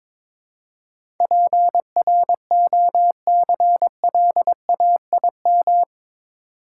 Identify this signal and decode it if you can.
Morse code, heard as a keyed tone, about 22 wpm, 700 Hz. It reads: PROCLAIM
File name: practice / 22wpm / Eight Letter Words - 22wpm